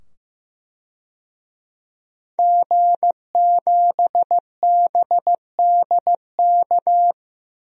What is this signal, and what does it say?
Morse code, heard as a keyed tone, about 15 wpm, 700 Hz. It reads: G7BDK
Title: DXCC-100 Call Signs - 5 Characters - 15wpm